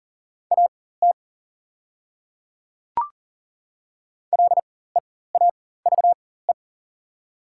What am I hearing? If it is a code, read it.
Morse code, heard as a keyed tone, about 40 wpm, spaced as for 19 wpm, 690 Hz. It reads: AT LEAVE
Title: 4x Character Spacing - Top 500 Words - 40wpm